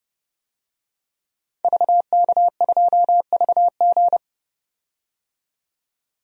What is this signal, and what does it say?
Morse code, heard as a keyed tone, about 32 wpm, 700 Hz. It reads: VK2VG
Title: DXCC-100 Call Signs - 5 Characters - 30wpm